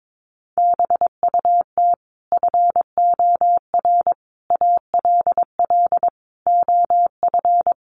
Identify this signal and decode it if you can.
Morse code, heard as a keyed tone, about 22 wpm, 700 Hz. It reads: BUT FOR ALL OF
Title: Sentences from Top 1300 Words 22wpm